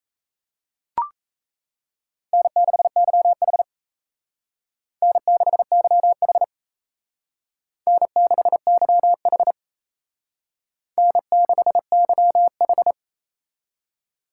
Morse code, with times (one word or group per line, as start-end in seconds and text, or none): 2.3-3.6 N6YH
5.0-6.4 N6YH
7.9-9.5 N6YH
11.0-12.9 N6YH